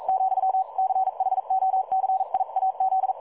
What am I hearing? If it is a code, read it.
Morse code, heard as a keyed tone, about 42 wpm, 760 Hz. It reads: , CHOCIAZ
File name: dlazaawansowanych / nagraniazpasma / websdr recording 3565.0kHz